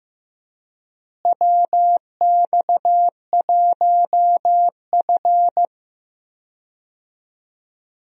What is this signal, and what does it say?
Morse code, heard as a keyed tone, about 15 wpm, 700 Hz. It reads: WX1F
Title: USA Call Signs - 4 Characters - Repeated 3x - 15wpm